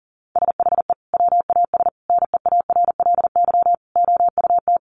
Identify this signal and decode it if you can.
Morse code, heard as a keyed tone, about 40 wpm, 700 Hz. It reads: SHE WAS NEARLY OUT
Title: Sentences from Top 1300 Words 40wpm